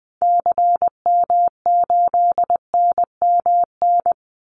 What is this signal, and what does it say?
Morse code, heard as a keyed tone, about 20 wpm, 700 Hz. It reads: CM8NMN
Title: Call Signs Repeated 3x - 20wpm